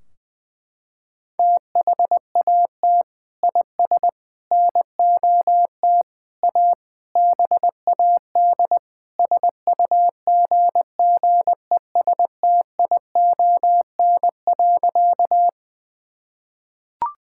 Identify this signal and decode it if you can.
Morse code, heard as a keyed tone, about 20 wpm, 700 Hz. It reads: THAT IS NOT A BAD SUGGESTION.
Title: Sentences from Top 1600 Words 20wpm